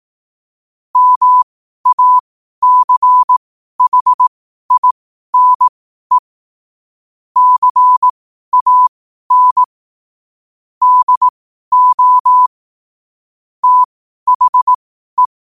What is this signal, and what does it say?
Morse code, heard as a keyed tone, about 18 wpm, spaced as for 12 wpm, 1000 Hz. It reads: MACHINE CAN DO THE
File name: qotdfiles / 12wpm